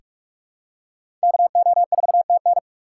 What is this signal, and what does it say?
Morse code, heard as a keyed tone, about 46 wpm, 700 Hz. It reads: KO4TN
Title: USA Call Signs - 5 Characters - 4x Character Spacing - 45wpm